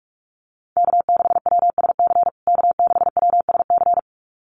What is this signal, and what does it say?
Morse code, heard as a keyed tone, about 44 wpm, 690 Hz. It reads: K6WSC K6WSC
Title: USA Call Signs Repeated 2x 45wpm